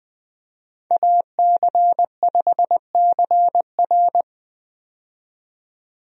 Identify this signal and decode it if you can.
Morse code, heard as a keyed tone, about 20 wpm, 700 Hz. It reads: AC5CR
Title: USA Call Signs - 5 Characters - 20wpm